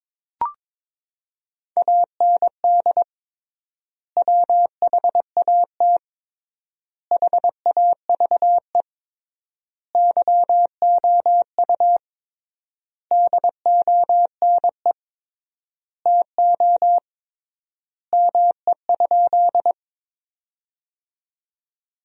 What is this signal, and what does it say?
Morse code, heard as a keyed tone, about 22 wpm, 700 Hz. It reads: AND WHAT HAVE YOU DONE TO ME?
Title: 3x Word-Spacing - Sentences from Top 500 Words - 22wpm